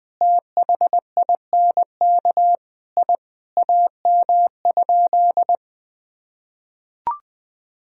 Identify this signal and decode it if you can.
Morse code, heard as a keyed tone, about 20 wpm, 700 Hz. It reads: THINK I AM?
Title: Sentences from Top 1000 Words - Review Part 3 - 20wpm